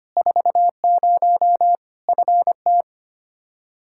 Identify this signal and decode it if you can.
Morse code, heard as a keyed tone, about 25 wpm, 700 Hz. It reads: 40 FT
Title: Practice QSO - Exchange 2 - Info - 25wpm